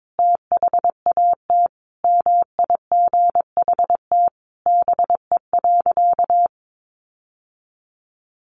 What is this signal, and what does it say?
Morse code, heard as a keyed tone, about 22 wpm, 700 Hz. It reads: THAT MIGHT BE.
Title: Sentences from Top 1500 Words 22wpm